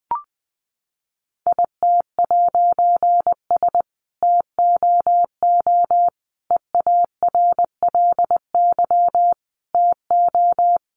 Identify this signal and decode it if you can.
Morse code, heard as a keyed tone, about 20 wpm, 700 Hz. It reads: IT'S TOO EARLY TO
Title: A Sound Of Thunder 20wpm